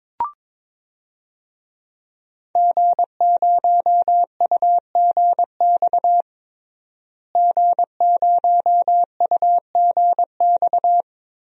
Morse code, heard as a keyed tone, about 22 wpm, 700 Hz. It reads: G0UGX G0UGX
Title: DXCC-100 Call Signs - 5 Characters - Repeated 2x - 22wpm